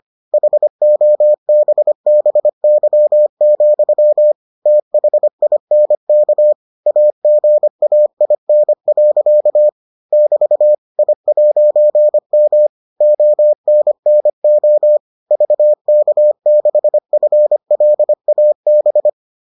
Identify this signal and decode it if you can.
Morse code, heard as a keyed tone, about 25 wpm, 600 Hz. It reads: HOBBY, THINK AGAIN. <BT> I'M ONNO VK6FLAB